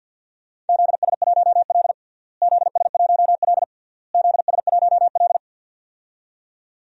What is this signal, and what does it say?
Morse code, heard as a keyed tone, about 51 wpm, 700 Hz. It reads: ZS1L ZS1L ZS1L
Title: DXCC-100 Call Signs - 4 Characters - Repeated 3x - 50wpm